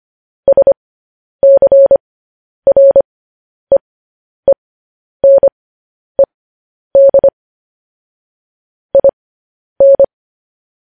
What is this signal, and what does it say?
Morse code, heard as a keyed tone, about 25 wpm, spaced as for 10 wpm, 550 Hz. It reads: SCREENED IN
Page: news headlines in morse code at 10 wpm